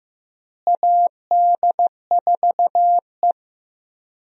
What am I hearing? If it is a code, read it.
Morse code, heard as a keyed tone, about 15 wpm, 700 Hz. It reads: AD4E
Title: USA Call Signs - 4 Characters - Repeated 2x - 15wpm